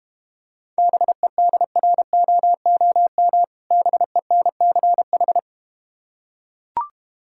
Morse code, heard as a keyed tone, about 32 wpm, 700 Hz. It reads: BEDROOM BENCH